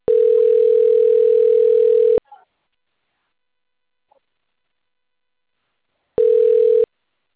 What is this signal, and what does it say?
Ringback tone (North America), 440 + 480 Hz.